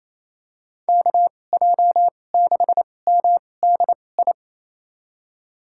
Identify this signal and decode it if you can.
Morse code, heard as a keyed tone, about 28 wpm, spaced as for 20 wpm, 700 Hz. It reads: KJ6MDI
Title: USA Call Signs - 6 Characters - 2x Character Spacing - 28wpm